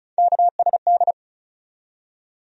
Morse code, heard as a keyed tone, about 34 wpm, 700 Hz. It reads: KSD
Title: USA Call Sign Postfixes - 3 Letters 35wpm